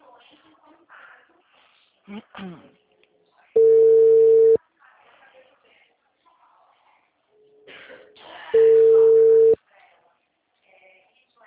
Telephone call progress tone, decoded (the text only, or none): ringback tone